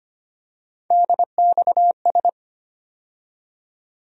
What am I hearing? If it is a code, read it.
Morse code, heard as a keyed tone, about 25 wpm, 700 Hz. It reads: DXS
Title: USA Call Sign Postfixes - 3 Letters 25wpm